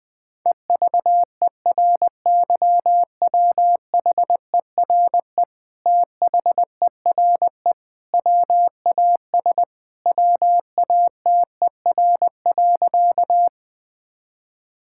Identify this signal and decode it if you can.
Morse code, heard as a keyed tone, about 20 wpm, 700 Hz. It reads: EVERYWHERE THERE WAS WATER.
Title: Sentences from Top 1600 Words 20wpm